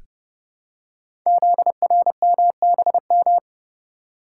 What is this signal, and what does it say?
Morse code, heard as a keyed tone, about 30 wpm, 700 Hz. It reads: ZRMBM
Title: Code Groups - 5 Letters - 30wpm